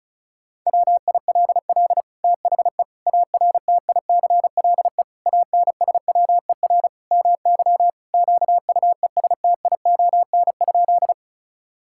Morse code, heard as a keyed tone, about 36 wpm, 700 Hz. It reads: WILL THE ARTICLE ANSWER MY QUESTION?
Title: Sentences from Top 1000 Words - Review Part 5 - 35wpm